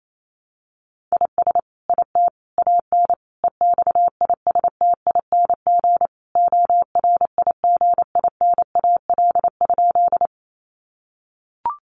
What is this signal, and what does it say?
Morse code, heard as a keyed tone, about 28 wpm, 700 Hz. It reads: IS IT AN EXISTING ORIGINAL?
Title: Sentences from Top 1500 Words 28wpm